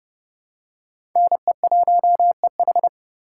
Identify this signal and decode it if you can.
Morse code, heard as a keyed tone, about 30 wpm, 700 Hz. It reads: NE1EH
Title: USA Call Signs - 3x Character Spacing - 30wpm